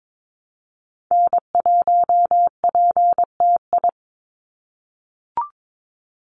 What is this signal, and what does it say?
Morse code, heard as a keyed tone, about 22 wpm, 700 Hz. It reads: N1PTI
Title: USA Call Signs 22wpm